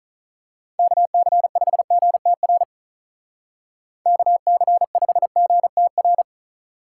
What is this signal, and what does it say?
Morse code, heard as a keyed tone, about 39 wpm, 700 Hz. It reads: KC5GTR KC5GTR
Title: Speed Racing - USA 2x3 Call Signs 35wpm